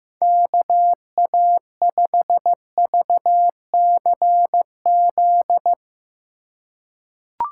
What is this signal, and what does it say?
Morse code, heard as a keyed tone, about 15 wpm, 700 Hz. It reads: KA5VCZ
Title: USA Call Signs - 6 Characters - 2x Character Spacing - 15wpm